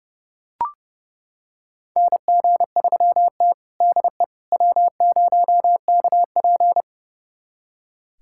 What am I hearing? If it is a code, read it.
Morse code, heard as a keyed tone, about 30 wpm, 700 Hz. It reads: NG3T DE W0KP
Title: USA Callsign DE Callsign - 4 Characters - 30wpm